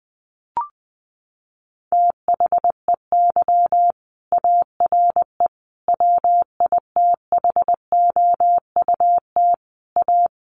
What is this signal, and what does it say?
Morse code, heard as a keyed tone, about 20 wpm, 700 Hz. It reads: THEY ARE WITHOUT A
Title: Sentences from Top 1300 Words 20wpm